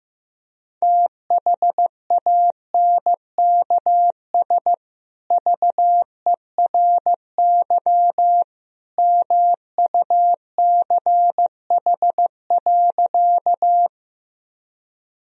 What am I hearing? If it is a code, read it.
Morse code, heard as a keyed tone, about 15 wpm, 700 Hz. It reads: THANKS VERY MUCH.
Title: Sentences from Top 1000 Words - Review Part 3 - 15wpm